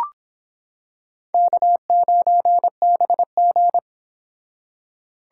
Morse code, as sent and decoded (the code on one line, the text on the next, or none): -.- ----. -... --.
K9BG